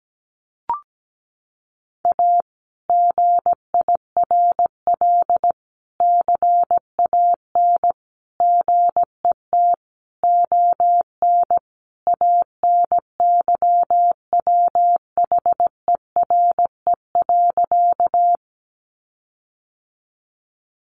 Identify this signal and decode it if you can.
Morse code, heard as a keyed tone, about 17 wpm, 700 Hz. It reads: A GIRL CAN GET ON ANYWHERE.
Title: Sentences from Top 1400 Words 17wpm